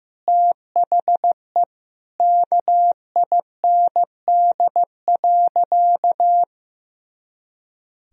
Morse code, heard as a keyed tone, about 16 wpm, 700 Hz. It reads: THE KIND.
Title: Sentences from Top 500 Words and Abbreviations 15wpm